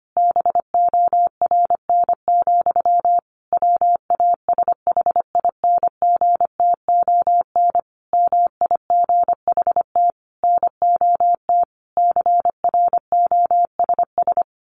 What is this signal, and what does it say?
Morse code, heard as a keyed tone, about 25 wpm, 700 Hz. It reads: BORN, WASHINGTON MIGHT NOT CROSS
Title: A Sound Of Thunder 25wpm